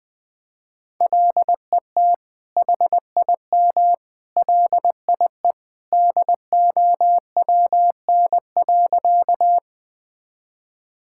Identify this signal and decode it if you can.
Morse code, heard as a keyed tone, about 20 wpm, 700 Hz. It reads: LET HIM LIE DOWN.
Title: Sentences from Top 1600 Words 20wpm